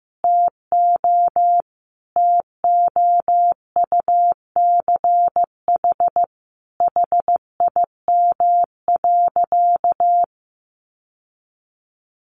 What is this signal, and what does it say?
Morse code, heard as a keyed tone, about 15 wpm, 700 Hz. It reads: TO TOUCH HIM.
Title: Sentences from Top 1200 Words 15wpm